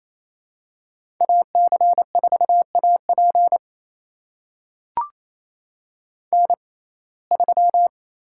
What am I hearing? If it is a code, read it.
Morse code, heard as a keyed tone, about 28 wpm, spaced as for 20 wpm, 700 Hz. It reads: AC4AP N3
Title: USA Call Signs - 6x Character Spacing - 28wpm